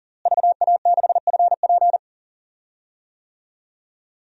Morse code, heard as a keyed tone, about 40 wpm, 700 Hz. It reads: VA6FP
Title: DXCC-100 Call Signs - 5 Characters - Repeated 2x - 40wpm